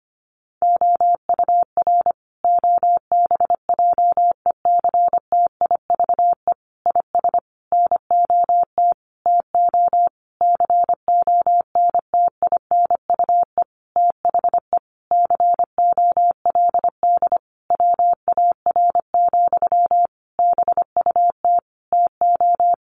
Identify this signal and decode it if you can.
Morse code, heard as a keyed tone, about 25 wpm, 700 Hz. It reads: OUR OBJECTIVE IS NOT TO CONTINUE THE COLD WAR, BUT TO
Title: Sentences from Top 1200 Words 25wpm